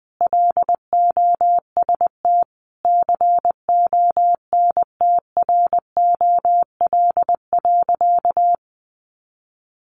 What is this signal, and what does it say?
Morse code, heard as a keyed tone, about 20 wpm, 700 Hz. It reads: LOST CONTROL.
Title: Sentences from Top 1400 Words 20wpm